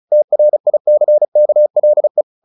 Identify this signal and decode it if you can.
Morse code, heard as a keyed tone, about 34 wpm, 600 Hz. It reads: TRICKLE
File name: the old man and the sea / the old man and the sea 1 35wpm